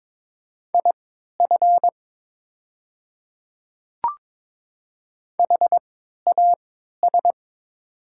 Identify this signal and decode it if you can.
Morse code, heard as a keyed tone, about 22 wpm, spaced as for 12 wpm, 700 Hz. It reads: IF HAS
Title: 3x Character Spacing - Top 100 Words - 22wpm